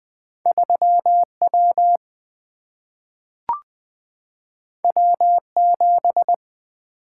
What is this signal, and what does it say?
Morse code, heard as a keyed tone, about 20 wpm, 700 Hz. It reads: 3 W W 7
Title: Single-Letter Single-Number 20wpm